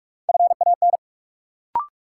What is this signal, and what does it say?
Morse code, heard as a keyed tone, about 45 wpm, 700 Hz. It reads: FAN
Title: Sentences from Top 1000 Words - Review Part 4 - New Word Drill - 45wpm